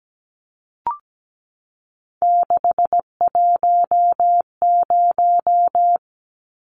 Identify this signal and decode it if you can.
Morse code, heard as a keyed tone, about 17 wpm, 700 Hz. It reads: 610
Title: Numbers - 3-Digit 17wpm